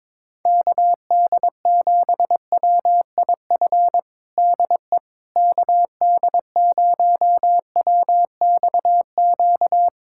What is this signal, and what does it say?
Morse code, heard as a keyed tone, about 22 wpm, 700 Hz. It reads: KD7WIF DE KD0WXQ
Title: USA Callsign DE Callsign - 6 Characters - 22wpm